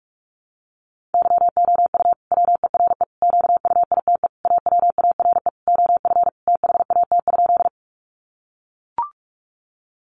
Morse code, heard as a keyed tone, about 44 wpm, 700 Hz. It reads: YOU WERE QUITE AWARE OF THAT?
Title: Sentences from Top 1200 Words 45wpm